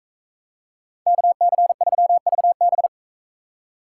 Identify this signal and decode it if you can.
Morse code, heard as a keyed tone, about 43 wpm, 700 Hz. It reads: KC3VB